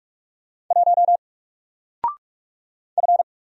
Morse code, heard as a keyed tone, about 44 wpm, 700 Hz. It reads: 1 F
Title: Single Letter-Number Alt 45wpm